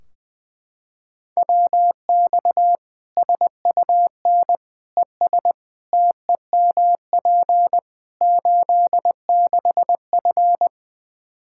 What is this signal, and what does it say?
Morse code, heard as a keyed tone, about 20 wpm, 700 Hz. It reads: WX SUN ES TEMP 86F
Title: K1USN SST QSO Practice 20wpm